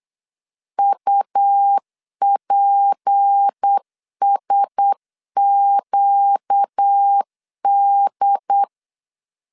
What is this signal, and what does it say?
Morse code, heard as a keyed tone, about 8 wpm, 800 Hz. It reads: UPSQD
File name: LEZIONI / 7 GF9Z / 7 GF9 8wpm 3545